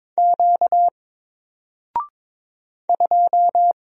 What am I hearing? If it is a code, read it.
Morse code, heard as a keyed tone, about 22 wpm, 700 Hz. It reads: Q 2
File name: Single-Letter-Number-22wpm